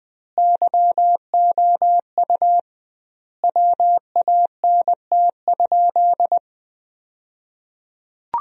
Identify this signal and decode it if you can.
Morse code, heard as a keyed tone, about 20 wpm, 700 Hz. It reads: YOU WANT?
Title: 2x Word-Spacing - Sentences from Top 200 Words - 20wpm